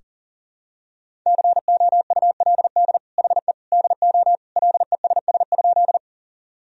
Morse code, heard as a keyed tone, about 40 wpm, 700 Hz. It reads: COULD HE DO LESS?
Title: Sentences from Top 1000 Words - Review Part 2 - 40wpm